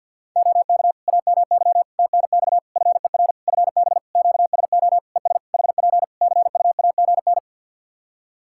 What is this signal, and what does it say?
Morse code, heard as a keyed tone, about 51 wpm, 700 Hz. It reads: OK AMY TNX FER FB QSO ES HP CUAGN
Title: Practice QSO - Exchange 4 - Ending - 50wpm